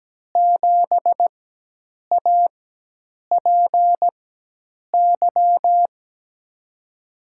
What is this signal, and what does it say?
Morse code, heard as a keyed tone, about 17 wpm, spaced as for 8 wpm, 700 Hz. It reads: 7APY
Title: USA Call Signs - 6 Characters - 4x Character Spacing - 17wpm